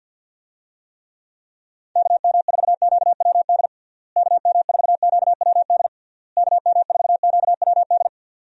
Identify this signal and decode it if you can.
Morse code, heard as a keyed tone, about 50 wpm, 700 Hz. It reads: KM4QWD KM4QWD KM4QWD
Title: USA Call Signs - 6 Characters - Repeated 3x - 50wpm